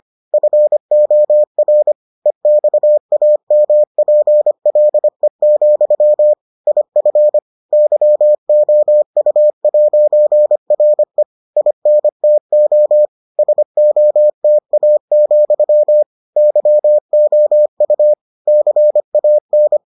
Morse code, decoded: FOR EXAMPLE, IF YOU'RE INTO SOTA, YOU CAN

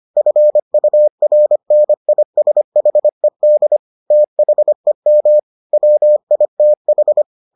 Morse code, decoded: FURNISHED THEM WITH

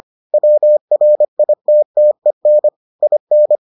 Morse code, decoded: WRITTEN IN

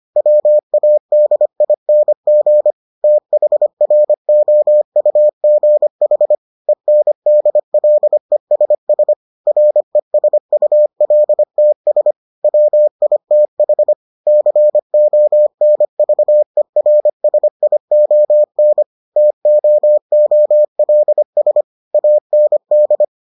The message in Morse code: WADING THROUGH ENDLESS RESULTS WITH CONVERSION TOOLS AND

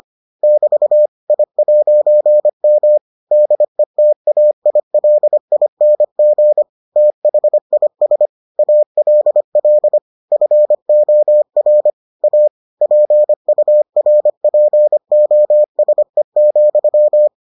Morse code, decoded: <BT> I'M DETAILING THIS ALL FOR A PURPOSE,

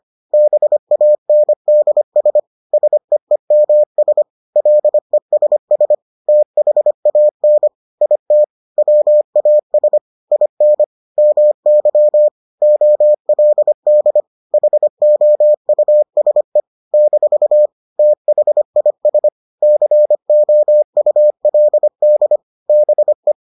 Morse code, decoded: BANDS SEEMS LESS THAN IT WAS IN MY OLD HOUSE - THIS COULD BE

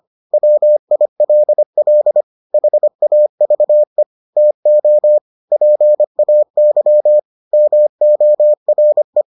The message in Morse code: WILL HAVE TO PAY MORE

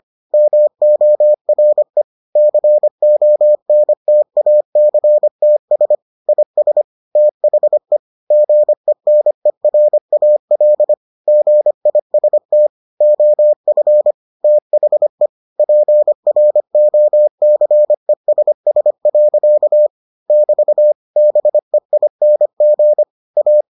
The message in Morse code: MORE CONTACTS IS THE GENERAL GIST OF THE PROCESS. <BT> BEING A

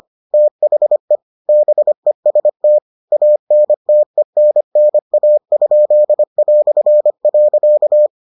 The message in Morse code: THE BEST ANTENNA?".

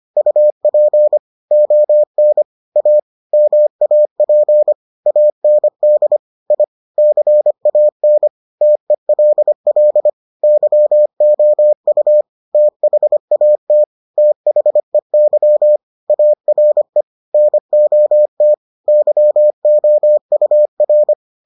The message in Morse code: UP ON A MAP AND I CAN TELL YOU THAT THEY ARE NOT YOUR

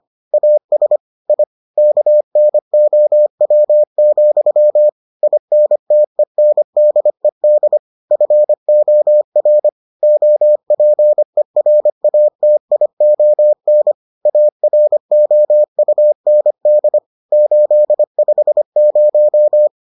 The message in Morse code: AS I KNOW, INTENDED FOR OPERATION AROUND 850

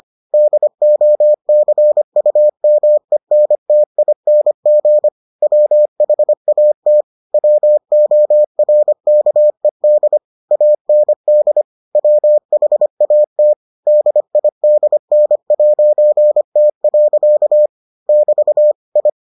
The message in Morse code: DOCUMENTING WHAT WORKED AND WHAT DIDN'T. <BT> I

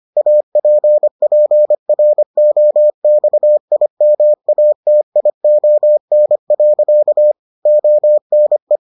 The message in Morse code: APPROXIMATION. ONE

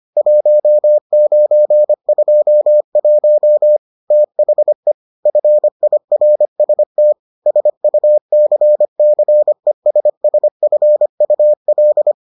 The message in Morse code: 1921 THE FIRST SUCCESSFUL

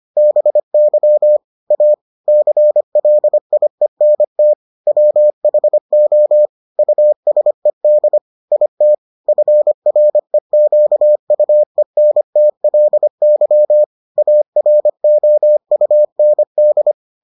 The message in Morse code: BY A CLIENT WHO USED IT FREQUENTLY AROUND

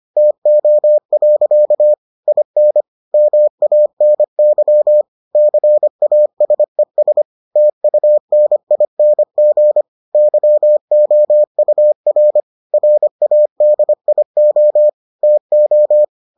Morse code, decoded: TO. IN MANY CASES TUNING YOUR RADIO TO

